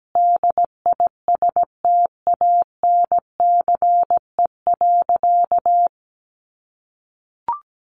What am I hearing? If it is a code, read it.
Morse code, heard as a keyed tone, about 17 wpm, 700 Hz. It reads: DISTANCE.
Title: Sentences from Top 1200 Words 17wpm